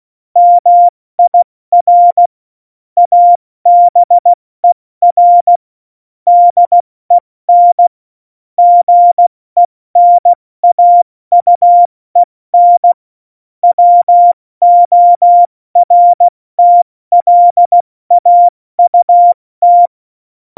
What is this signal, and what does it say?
Morse code, heard as a keyed tone, about 16 wpm, 700 Hz. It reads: MIR ABER DEN GENAUEN WORTLAUT
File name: Morsekurs / Witze / Witz 30. Juli 2019, 70 F80